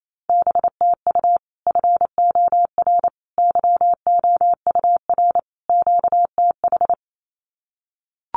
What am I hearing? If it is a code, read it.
Morse code, heard as a keyed tone, about 28 wpm, 700 Hz. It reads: BTU FOR YOUR QTH
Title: Common QSO Phrases 28wpm